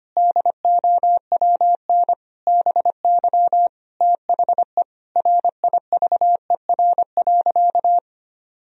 Morse code, heard as a keyed tone, about 25 wpm, 700 Hz. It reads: DOWN BY THE RIVER.
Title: Sentences from Top 1000 Words - Review Part 1 - 25wpm